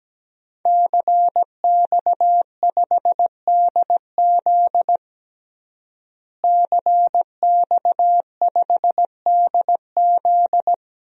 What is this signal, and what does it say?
Morse code, heard as a keyed tone, about 17 wpm, 700 Hz. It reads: CX5DZ CX5DZ
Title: DXCC-100 Call Signs - 5 Characters - Repeated 3x - 17wpm